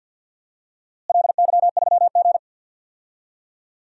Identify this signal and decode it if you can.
Morse code, heard as a keyed tone, about 50 wpm, 700 Hz. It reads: LY3G
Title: DXCC-100 Call Signs - 4 Characters - Repeated 2x - 50wpm